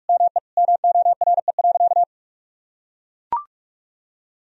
Morse code, heard as a keyed tone, about 41 wpm, 700 Hz. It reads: ME MORE.
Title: Sentences from Top 1000 Words - Review Part 1 - 45wpm